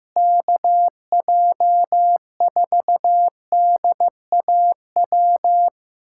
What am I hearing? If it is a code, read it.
Morse code, heard as a keyed tone, about 15 wpm, 700 Hz. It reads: KJ4DAW
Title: USA Callsign DE Callsign - 6 Characters - 15wpm